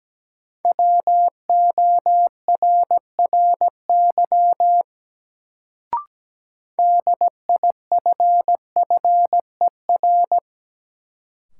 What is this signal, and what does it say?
Morse code, heard as a keyed tone, about 17 wpm, 700 Hz. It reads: WORRY DIFFER
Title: Sentences from Top 1100 Words - New Word Drill Alt - 17wpm